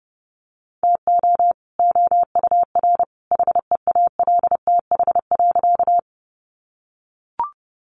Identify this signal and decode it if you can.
Morse code, heard as a keyed tone, about 30 wpm, 700 Hz. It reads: TO OUR HEALTH.
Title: Sentences from Top 600 Words 30wpm